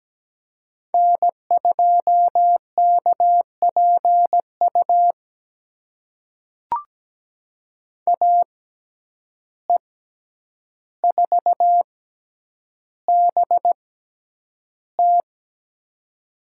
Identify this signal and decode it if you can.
Morse code, heard as a keyed tone, about 17 wpm, spaced as for 9 wpm, 700 Hz. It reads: N2KPU AE4BT
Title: USA Call Signs - 5 Characters - 6x Character Spacing - 17wpm